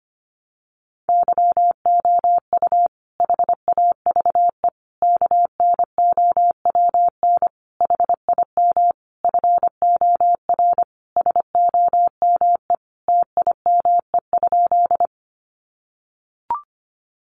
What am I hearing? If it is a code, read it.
Morse code, heard as a keyed tone, about 25 wpm, 700 Hz. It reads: YOU HAVE KNOWN HIM FOR SOME TIME?
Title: Sentences from Top 600 Words 25wpm